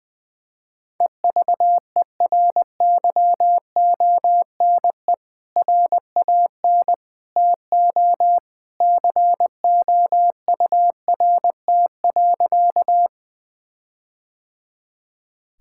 Morse code, heard as a keyed tone, about 20 wpm, 700 Hz. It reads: EVERYONE RAN TO COURT.
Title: Sentences from Top 1600 Words 20wpm